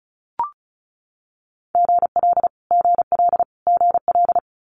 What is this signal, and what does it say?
Morse code, heard as a keyed tone, about 36 wpm, 700 Hz. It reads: GL GL GL
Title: Multi-letter Phonemes Repeated 3x 35wpm